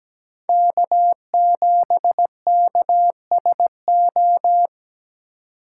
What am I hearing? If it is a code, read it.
Morse code, heard as a keyed tone, about 20 wpm, 700 Hz. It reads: K7KSO
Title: Speed Racing - USA 1x3 Call Signs 17wpm